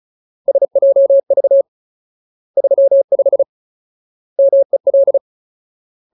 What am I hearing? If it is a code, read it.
Morse code, heard as a keyed tone, about 34 wpm, 550 Hz. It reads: SJV 35 MEL
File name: Ultimate Word List / 35WPM Ultimate